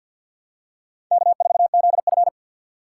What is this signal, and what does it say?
Morse code, heard as a keyed tone, about 51 wpm, 700 Hz. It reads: K4ZF
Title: USA Call Signs - 2x Character Spacing - 50wpm